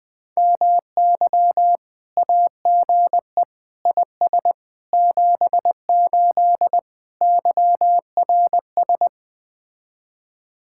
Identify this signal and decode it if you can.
Morse code, heard as a keyed tone, about 21 wpm, 700 Hz. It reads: MY AGE IS 78 YRS